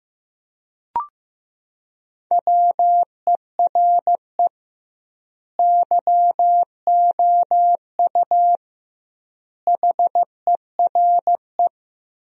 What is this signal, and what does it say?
Morse code, heard as a keyed tone, about 15 wpm, 700 Hz. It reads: WERE YOU HERE
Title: 2x Word-Spacing - Sentences from Top 200 Words - 15wpm